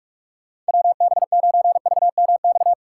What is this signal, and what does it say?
Morse code, heard as a keyed tone, about 54 wpm, 700 Hz. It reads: WB9VMX
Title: Speed Racing - USA 2x3 Call Signs 45wpm